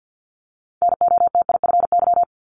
Morse code, heard as a keyed tone, about 50 wpm, 690 Hz. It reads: NOTIFY
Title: Multi-letter Phonemes 50wpm